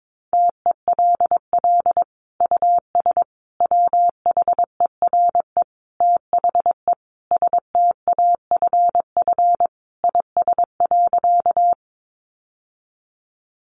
Morse code, heard as a keyed tone, about 22 wpm, 700 Hz. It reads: TELL US WHERE THE STAFF IS.
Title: Sentences from Top 1200 Words 22wpm